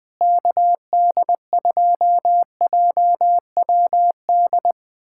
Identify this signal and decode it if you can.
Morse code, heard as a keyed tone, about 21 wpm, 700 Hz. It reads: KD2JWD
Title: USA Callsign DE Callsign 20wpm